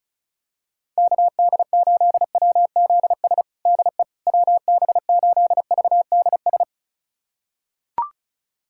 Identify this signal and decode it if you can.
Morse code, heard as a keyed tone, about 34 wpm, 700 Hz. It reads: KD8WZS DE WB8VDS
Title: USA Callsign DE Callsign - 6 Characters - 35wpm